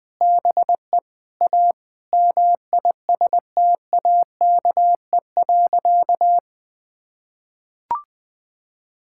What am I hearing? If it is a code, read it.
Morse code, heard as a keyed tone, about 21 wpm, 700 Hz. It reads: BE A MISTAKE.
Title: Sentences with 4-Word Phrases - 20wpm